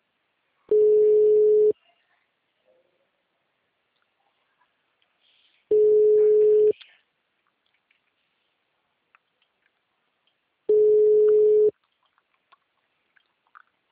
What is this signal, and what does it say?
Ringback tone (Europe, 425 Hz), with speech.
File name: out- -2025- - - .15209